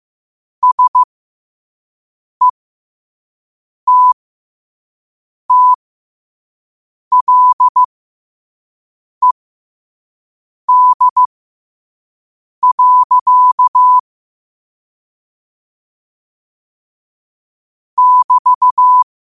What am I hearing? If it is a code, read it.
Morse code, heard as a keyed tone, about 15 wpm, spaced as for 5 wpm, 1000 Hz. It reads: SETTLED. <BT>
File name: qotdfiles / 5wpm